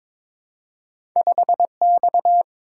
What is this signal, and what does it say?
Morse code, heard as a keyed tone, about 22 wpm, 700 Hz. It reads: 5X
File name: practice / 22wpm / Two Letter-Number Combinations - 22wpm